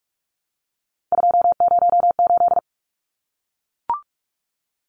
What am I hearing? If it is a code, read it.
Morse code, heard as a keyed tone, about 44 wpm, 700 Hz. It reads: 208
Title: Numbers - 3-Digit 45wpm